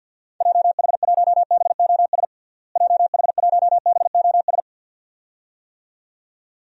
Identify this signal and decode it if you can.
Morse code, heard as a keyed tone, about 52 wpm, 700 Hz. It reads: JH1BOS JH1BOS
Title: DXCC-100 Call Signs - 6 Characters - Repeated 2x - 50wpm